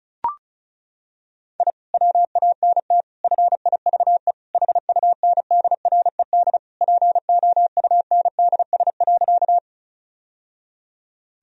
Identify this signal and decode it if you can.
Morse code, heard as a keyed tone, about 34 wpm, 700 Hz. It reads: I WANT FIVE HUNDRED POUNDS.
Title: Sentences from Top 1000 Words - Review Part 2 - 35wpm